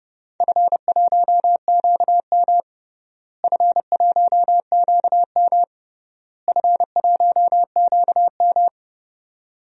Morse code, heard as a keyed tone, about 32 wpm, 700 Hz. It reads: F1QM F1QM F1QM
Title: DXCC-100 Call Signs - 4 Characters - Repeated 3x - 30wpm